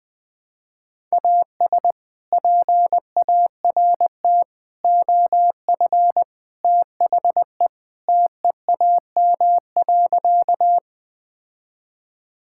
Morse code, heard as a keyed tone, about 21 wpm, 700 Hz. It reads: AS PART OF THE TEAM.
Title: Sentences with 3-Word Phrases - 20wpm